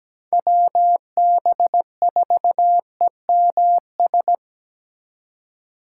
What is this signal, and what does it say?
Morse code, heard as a keyed tone, about 17 wpm, 700 Hz. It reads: WB4EMS
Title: USA Call Signs - 2x Character Spacing - 17wpm